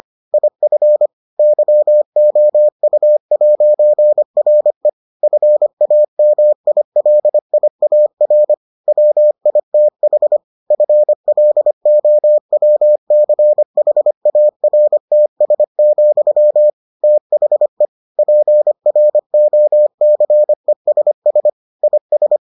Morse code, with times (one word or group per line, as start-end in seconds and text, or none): 0.3-1.1 IF
1.4-4.9 YOU'RE
5.2-8.5 FAMILIAR
8.9-10.4 WITH
10.7-16.7 FLOWCHARTS,
17.0-17.9 THE
18.2-21.5 PROCESS
21.8-22.4 IS